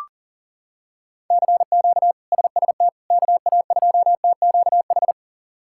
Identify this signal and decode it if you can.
Morse code, heard as a keyed tone, about 40 wpm, 700 Hz. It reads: CQ SST KA2TQH
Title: K1USN SST QSO Practice - 40wpm